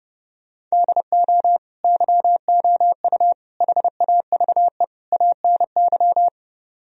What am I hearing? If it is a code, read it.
Morse code, heard as a keyed tone, about 31 wpm, 700 Hz. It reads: DO YOU HAVE ANY